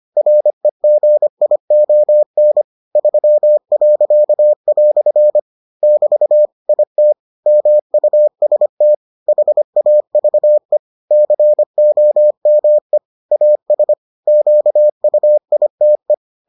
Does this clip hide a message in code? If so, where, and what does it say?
Yes: Morse code from 0.2 to 16.1 s, heard as a keyed tone, about 25 wpm, 600 Hz. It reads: REGION 3." <BT> IT MUST HAVE COME AS QUITE